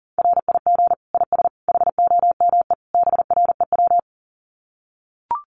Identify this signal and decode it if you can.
Morse code, heard as a keyed tone, about 40 wpm, 700 Hz. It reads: RIG IS HOME BREW
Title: Common QSO Phrases 40wpm